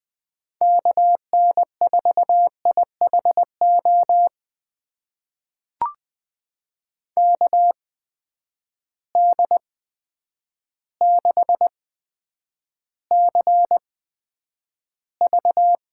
Morse code, heard as a keyed tone, about 20 wpm, spaced as for 9 wpm, 700 Hz. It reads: KN4IHO KD6CV
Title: USA Call Signs - 6 Characters - 8x Character Spacing - 20wpm